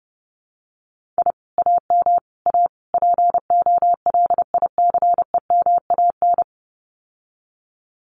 Morse code, heard as a keyed tone, about 32 wpm, 700 Hz. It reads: I AM A POLICEMAN
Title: Common QSO Phrases Encore 30wpm